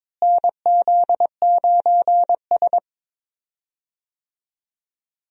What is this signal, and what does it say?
Morse code, heard as a keyed tone, about 21 wpm, 700 Hz. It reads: NZ9S
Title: USA Call Signs - 4 Characters - Repeated 3x - 22wpm